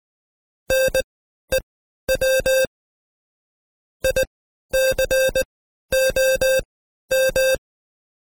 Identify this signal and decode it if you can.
Morse code, heard as a keyed tone, about 19 wpm, spaced as for 12 wpm, 530 Hz. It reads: NEW ICOM
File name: Code Quick / AudioDownld2 / Seg132-Stepper5 11WPM